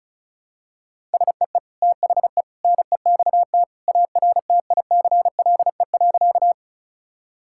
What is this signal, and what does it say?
Morse code, heard as a keyed tone, about 34 wpm, 700 Hz. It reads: SEE THE NEXT ARTICLE.
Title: Sentences from Top 1000 Words - Review Part 5 - 35wpm